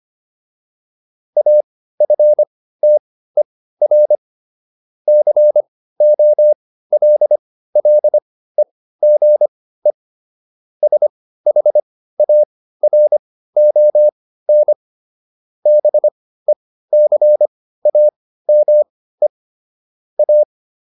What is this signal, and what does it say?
Morse code, heard as a keyed tone, about 25 wpm, spaced as for 15 wpm, 600 Hz. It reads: AFTER COLLEGE SHARON BECAME A